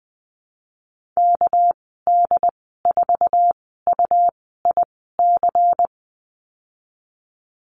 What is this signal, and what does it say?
Morse code, heard as a keyed tone, about 20 wpm, spaced as for 15 wpm, 700 Hz. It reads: KD4UIC
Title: 207 Lesson 33g - Letter j - Call Signs 20wpm